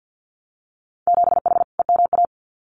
Morse code, heard as a keyed tone, about 50 wpm, 700 Hz. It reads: 7H EMA
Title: Winter Field Day Exchanges Encore 50wpm